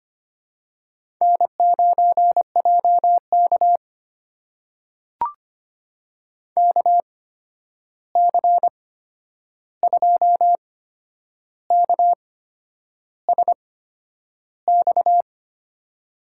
Morse code, heard as a keyed tone, about 25 wpm, spaced as for 9 wpm, 700 Hz. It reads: N9JK KC2KSX
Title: USA Call Signs - 8x Character Spacing - 25wpm